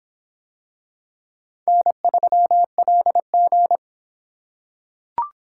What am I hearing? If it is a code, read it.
Morse code, heard as a keyed tone, about 26 wpm, 700 Hz. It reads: N3LG